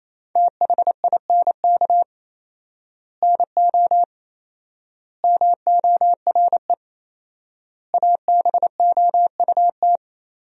Morse code, heard as a keyed tone, about 28 wpm, 700 Hz. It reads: THINK NO MORE ABOUT
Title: 4x Word-Spacing - Sentences from Top 200 Words - 28wpm